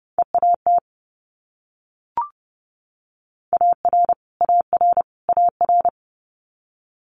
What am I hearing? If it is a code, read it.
Morse code, heard as a keyed tone, about 31 wpm, 700 Hz. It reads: EAT AR AR AR
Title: Multi-letter Phonemes Repeated 3x 30wpm